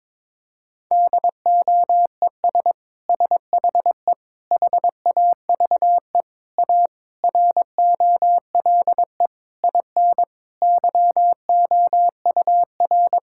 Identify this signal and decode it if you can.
Morse code, heard as a keyed tone, about 21 wpm, 700 Hz. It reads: DOES SHE HAVE A ROLE IN YOUR
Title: Sentences from Top 1000 Words - Review Part 6 - 22wpm